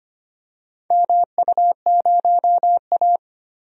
Morse code, heard as a keyed tone, about 25 wpm, 700 Hz. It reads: MU0A
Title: DXCC-100 Call Signs - 4 Characters - Repeated 2x - 25wpm